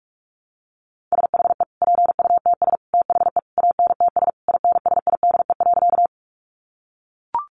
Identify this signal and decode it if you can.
Morse code, heard as a keyed tone, about 44 wpm, 700 Hz. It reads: SHE PUTS THE ANTS INSIDE.
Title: Additional Easy Phrases 45wpm